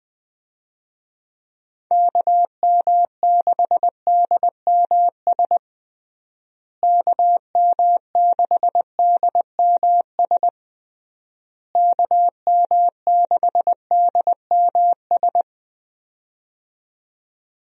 Morse code, heard as a keyed tone, about 20 wpm, 700 Hz. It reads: KM6DMS KM6DMS KM6DMS
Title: USA Call Signs - 6 Characters - Repeated 3x - 20wpm